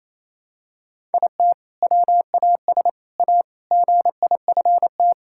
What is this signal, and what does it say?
Morse code, heard as a keyed tone, about 28 wpm, 700 Hz. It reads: IT WAS A GIFT
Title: Sentences from Top 1000 Words - Review Part 6 - 28wpm